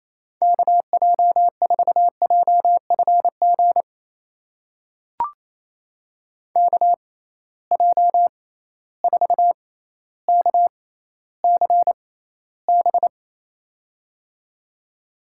Morse code, heard as a keyed tone, about 28 wpm, spaced as for 14 wpm, 700 Hz. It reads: KJ4JFG KJ4KCB
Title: USA Call Signs - 6 Characters - 6x Character Spacing - 28wpm